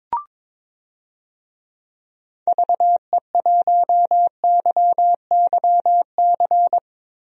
Morse code, heard as a keyed tone, about 22 wpm, 700 Hz. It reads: VE1YYC
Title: DXCC-100 Call Signs - 6 Characters - 22wpm